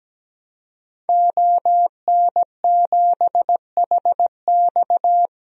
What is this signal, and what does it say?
Morse code, heard as a keyed tone, about 17 wpm, 700 Hz. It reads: ON7HX
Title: DXCC-100 Call Signs - 5 Characters - 17wpm